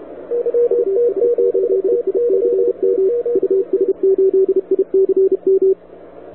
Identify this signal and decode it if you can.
Morse code, heard as a keyed tone, about 32 wpm, 380 Hz, with radio noise. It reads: US8ICM US8ICM
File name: wwcw08 / k5zd wwcw08 2000z